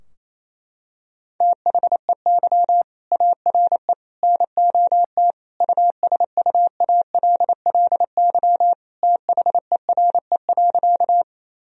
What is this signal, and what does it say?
Morse code, heard as a keyed tone, about 28 wpm, 700 Hz. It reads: THEY ARE NOT USUALLY THERE.
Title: Sentences from Top 1000 Words - Review Part 3 - 28wpm